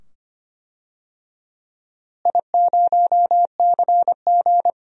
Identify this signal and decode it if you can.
Morse code, heard as a keyed tone, about 25 wpm, 700 Hz. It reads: I0CG
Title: DXCC-100 Call Signs - 4 Characters - Repeated 2x - 25wpm